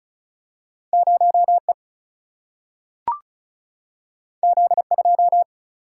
Morse code, heard as a keyed tone, about 36 wpm, 700 Hz. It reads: 0E Z2
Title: Single-Letter Single-Number 35wpm